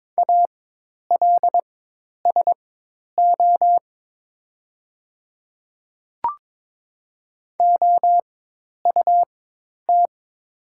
Morse code, heard as a keyed tone, about 22 wpm, spaced as for 10 wpm, 700 Hz. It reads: ALSO OUT
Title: 4x Character Spacing - Top 200 Words - 22wpm